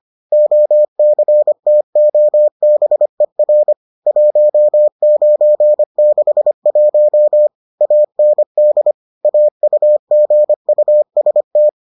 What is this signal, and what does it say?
Morse code, heard as a keyed tone, about 25 wpm, 600 Hz. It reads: OCTOBER 1961 AND AUGUST